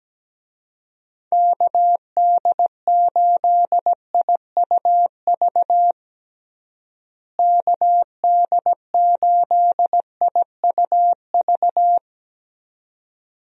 Morse code, heard as a keyed tone, about 17 wpm, 700 Hz. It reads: KD8IUV KD8IUV
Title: USA Call Signs - 6 Characters - Repeated 2x - 17wpm